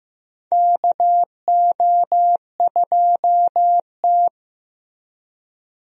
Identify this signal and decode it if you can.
Morse code, heard as a keyed tone, about 15 wpm, 700 Hz. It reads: KO2T
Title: USA Call Signs - 4 Characters - Repeated 3x - 15wpm